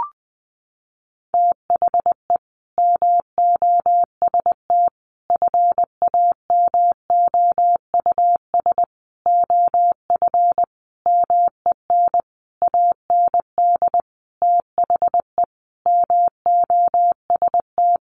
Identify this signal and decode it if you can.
Morse code, heard as a keyed tone, about 20 wpm, 700 Hz. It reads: THE MOST FAMOUS OF MEN AND THE MOST
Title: Sentences from Top 1500 Words 20wpm